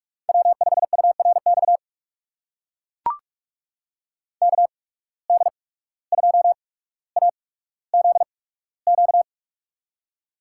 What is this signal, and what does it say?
Morse code, heard as a keyed tone, about 42 wpm, spaced as for 18 wpm, 700 Hz. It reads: W5URX KD2AZQ
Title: USA Call Signs - 8x Character Spacing - 45wpm